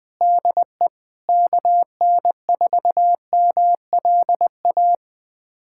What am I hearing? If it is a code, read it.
Morse code, heard as a keyed tone, about 20 wpm, 700 Hz. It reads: DE KN4MLA
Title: USA Callsign DE Callsign - 6 Characters - 20wpm